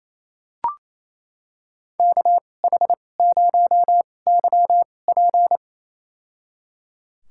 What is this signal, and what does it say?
Morse code, heard as a keyed tone, about 28 wpm, spaced as for 20 wpm, 700 Hz. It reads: KH0YP
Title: USA Call Signs - 5 Characters - 2x Character Spacing - 28wpm